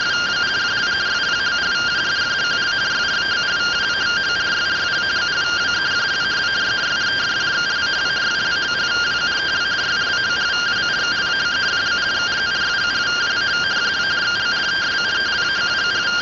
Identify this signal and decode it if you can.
Two-tone teleprinter signal, about 170 Hz shift.
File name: findings / CT AXIAL -142.gen-14 rtty rx